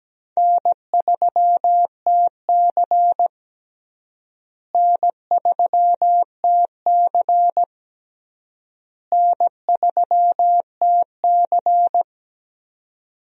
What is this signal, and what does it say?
Morse code, heard as a keyed tone, about 17 wpm, 700 Hz. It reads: N3TC N3TC N3TC
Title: USA Call Signs - 4 Characters - Repeated 3x - 17wpm